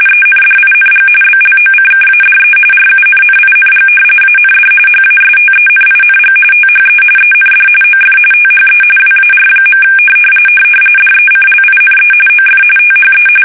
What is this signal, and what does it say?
RTTY, about 45.45 baud. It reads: ENDING BLOCKADE RUNNING AND HASTENING THE END OF THE WAR. DECLINE OF THE
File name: fsk850 / usnavy3